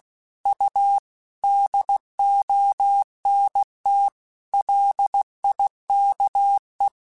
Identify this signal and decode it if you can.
Morse code, heard as a keyed tone, about 16 wpm, 800 Hz. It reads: U DONT LIKE
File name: QSOs / QSOs-mp3s / WD0BGP 2@15